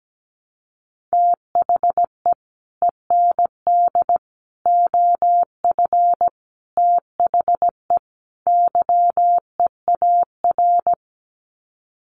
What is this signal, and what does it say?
Morse code, heard as a keyed tone, about 17 wpm, 700 Hz. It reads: THE END OF THE YEAR
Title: Sets of 5 Words 17wpm